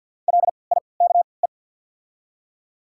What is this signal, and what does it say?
Morse code, heard as a keyed tone, about 53 wpm, spaced as for 29 wpm, 690 Hz. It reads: LIKE
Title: 3x Character Spacing - Top 100 Words - 50wpm